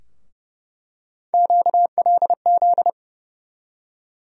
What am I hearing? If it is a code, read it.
Morse code, heard as a keyed tone, about 30 wpm, 700 Hz. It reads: QLZ
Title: USA Call Sign Postfixes - 3 Letters 30wpm